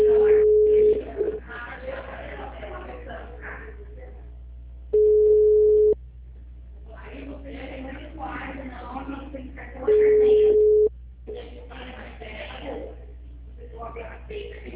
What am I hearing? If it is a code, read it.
Ringback tone (Europe, 425 Hz), with speech.